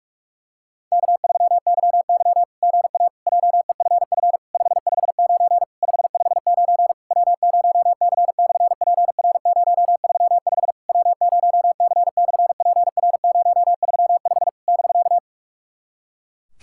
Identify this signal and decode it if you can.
Morse code, heard as a keyed tone, about 46 wpm, 700 Hz. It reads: K3YY GA JEFF 559 559 W0C/PR035 W0C/PR035 <BK>
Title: SOTA QSO Practice 45wpm